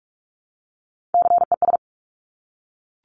Morse code, heard as a keyed tone, about 46 wpm, 700 Hz. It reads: CES
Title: Top Three Letter Combinations in Words 45wpm